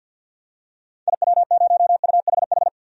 Morse code, heard as a keyed tone, about 52 wpm, 700 Hz. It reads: IW0UHH
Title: DXCC-100 Call Signs - 6 Characters - 50wpm